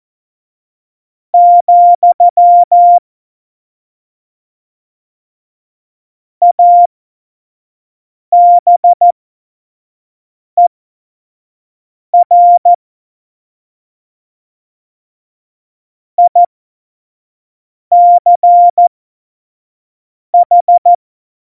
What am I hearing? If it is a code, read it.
Morse code, heard as a keyed tone, about 14 wpm, spaced as for 5 wpm, 700 Hz. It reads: , ABER ICH
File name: Morsekurs / Witze / Witz 28. August 2018, 25 F70